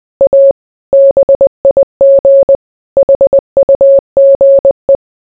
Morse code, heard as a keyed tone, about 20 wpm, 550 Hz. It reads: A BIG HUGE